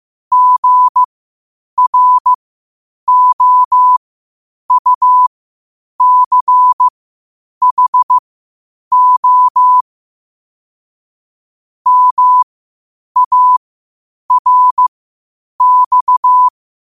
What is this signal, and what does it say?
Morse code, heard as a keyed tone, about 15 wpm, spaced as for 8 wpm, 1000 Hz. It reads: GROUCHO MARX